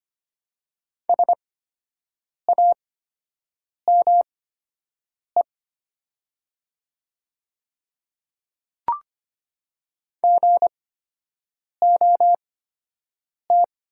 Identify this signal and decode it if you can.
Morse code, heard as a keyed tone, about 25 wpm, spaced as for 7 wpm, 700 Hz. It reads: SAME GOT
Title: 8x Character Spacing - Top 300 Words - 25wpm